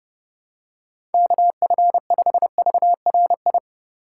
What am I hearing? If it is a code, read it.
Morse code, heard as a keyed tone, about 32 wpm, 700 Hz. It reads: KF5VRI